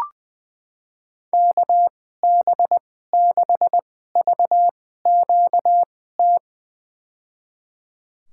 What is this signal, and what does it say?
Morse code, heard as a keyed tone, about 19 wpm, spaced as for 14 wpm, 700 Hz. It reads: KB6VQT
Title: USA Call Signs - 6 Characters - 2x Character Spacing - 20wpm